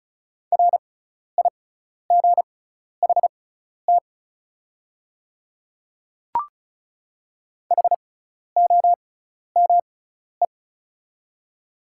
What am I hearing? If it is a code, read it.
Morse code, heard as a keyed tone, about 36 wpm, spaced as for 12 wpm, 690 Hz. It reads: RIGHT HOME
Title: 6x Character Spacing - Top 200 Words - 35wpm